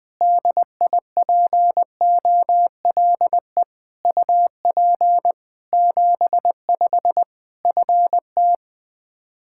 Morse code, heard as a keyed tone, about 20 wpm, 700 Hz. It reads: DIPOLE UP 75 FT
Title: Practice QSO - Exchange 2 - Info - 20wpm